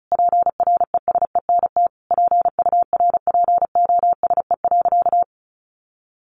Morse code, heard as a keyed tone, about 34 wpm, 700 Hz. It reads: PRESENT PURPOSE.
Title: Sentences from Top 1400 Words 35wpm